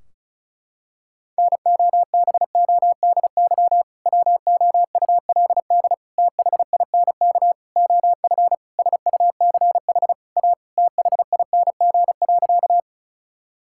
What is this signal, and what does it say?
Morse code, heard as a keyed tone, about 34 wpm, 700 Hz. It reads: NOBODY WOULD THINK OF SUCH A THING.
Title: Sentences from Top 1600 Words 35wpm